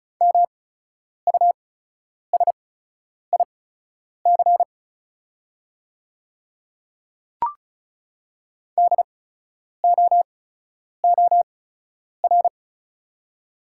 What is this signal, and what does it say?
Morse code, heard as a keyed tone, about 34 wpm, spaced as for 10 wpm, 700 Hz. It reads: MUSIC DOOR
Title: 8x Character Spacing - Top 300 Words - 35wpm